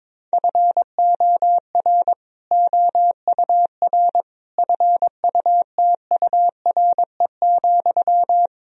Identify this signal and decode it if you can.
Morse code, heard as a keyed tone, about 22 wpm, 700 Hz. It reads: FOR OUR FUTURE,
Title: Sentences from Top 1000 Words - Review Part 3 - 22wpm